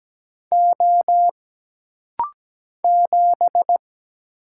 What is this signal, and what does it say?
Morse code, heard as a keyed tone, about 17 wpm, 700 Hz. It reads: O 7
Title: Single Letter-Number Alt - Rapid-Fire 17wpm